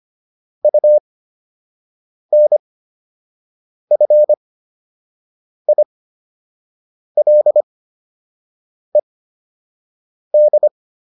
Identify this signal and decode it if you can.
Morse code, heard as a keyed tone, about 25 wpm, spaced as for 6 wpm, 600 Hz. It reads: UNFILED